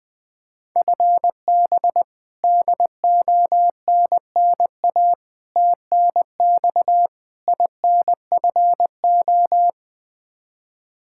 Morse code, heard as a keyed tone, about 20 wpm, 700 Hz. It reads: FB DONNA TNX INFO
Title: Practice QSO - Exchange 3 - More Info - 20wpm